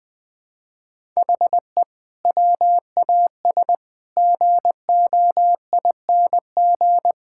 Morse code, heard as a keyed tone, about 20 wpm, 700 Hz. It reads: HE WAS GOING
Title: Sentences with 5-Word Phrases - Spoken Beginning - 20wpm